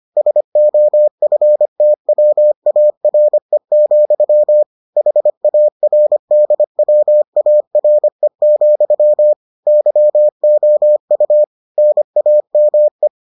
Morse code, heard as a keyed tone, about 25 wpm, 600 Hz. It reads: SOFTWARE, HARDWARE, YOU NAME